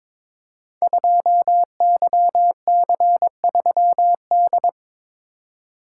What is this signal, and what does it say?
Morse code, heard as a keyed tone, about 23 wpm, 700 Hz. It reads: 2YC3D
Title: Code Groups - 5 Letter:Numbers - 22wpm